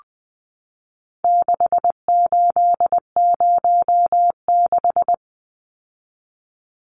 Morse code, heard as a keyed tone, about 20 wpm, 700 Hz. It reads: 6806
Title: Numbers - 4-Digit 20wpm